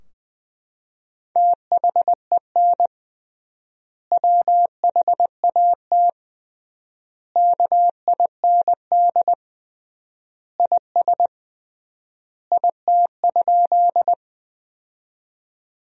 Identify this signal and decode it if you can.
Morse code, heard as a keyed tone, about 20 wpm, 700 Hz. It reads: THEN WHAT KIND IS IT?
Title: 3x Word-Spacing - Sentences from Top 200 Words - 20wpm